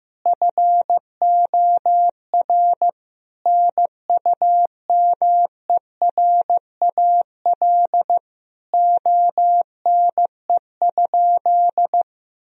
Morse code, heard as a keyed tone, about 15 wpm, 700 Hz. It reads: FOR NUMERAL ONE?
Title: Sentences from Top 1000 Words - Review Part 2 - 15wpm